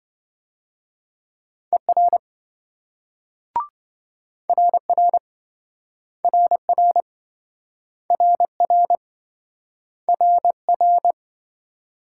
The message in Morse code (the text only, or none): ER RR RR RR RR